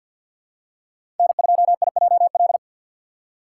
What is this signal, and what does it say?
Morse code, heard as a keyed tone, about 53 wpm, 700 Hz. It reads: N2IJL
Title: USA Call Signs - 5 Characters - 3x Character Spacing - 50wpm